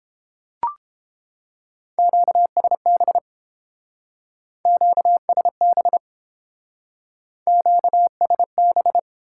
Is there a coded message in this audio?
Yes: Morse code from 2.0 to 9.0 s, heard as a keyed tone, about 29 wpm, 700 Hz. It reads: QSB QSB QSB